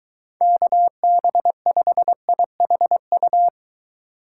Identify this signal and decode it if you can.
Morse code, heard as a keyed tone, about 23 wpm, 700 Hz. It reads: KB5IHU